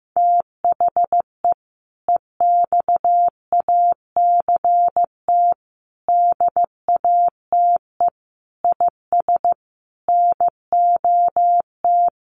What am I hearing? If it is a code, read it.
Morse code, heard as a keyed tone, about 15 wpm, 700 Hz. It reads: THE EXACT DATE IS NOT
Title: Sentences from Top 1300 Words 15wpm